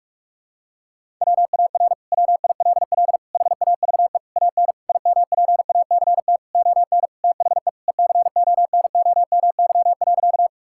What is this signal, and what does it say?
Morse code, heard as a keyed tone, about 42 wpm, 700 Hz. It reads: WAR WILL HAVE AN IMPACT ON THE ECONOMY.
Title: Sentences from Top 1000 Words - Review Part 6 - 45wpm